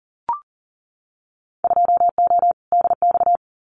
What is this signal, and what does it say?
Morse code, heard as a keyed tone, about 40 wpm, 700 Hz. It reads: 2O DX
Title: Winter Field Day Exchanges Encore 40wpm